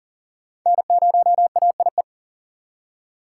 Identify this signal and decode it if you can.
Morse code, heard as a keyed tone, about 37 wpm, 700 Hz. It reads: N0AIE
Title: K1USN SST QSO Practice - 40wpm